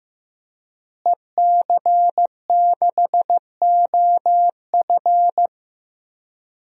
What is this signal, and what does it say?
Morse code, heard as a keyed tone, about 15 wpm, 700 Hz. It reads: EC6OF
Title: DXCC-100 Call Signs - 5 Characters - Repeated 2x - 15wpm